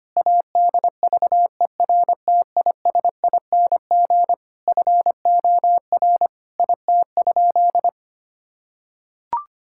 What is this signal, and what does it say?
Morse code, heard as a keyed tone, about 25 wpm, 700 Hz. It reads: ADVERTISING FOR IT?
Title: Sentences from Top 1000 Words - Review Part 6 - 25wpm